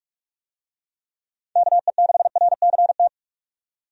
Morse code, heard as a keyed tone, about 46 wpm, 700 Hz. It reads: KE6RCT
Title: USA Call Signs - 6 Characters - 45wpm